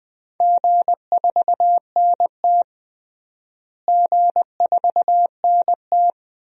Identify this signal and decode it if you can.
Morse code, heard as a keyed tone, about 20 wpm, 700 Hz. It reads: G4NT G4NT
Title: DXCC-100 Call Signs - 4 Characters - Repeated 2x - 20wpm